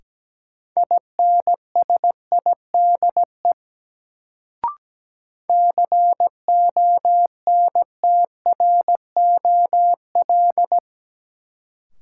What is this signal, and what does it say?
Morse code, heard as a keyed tone, about 17 wpm, 700 Hz. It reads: INSIDE CONTROL
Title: Sentences from Top 1200 Words - New Word Drill Alt - 17wpm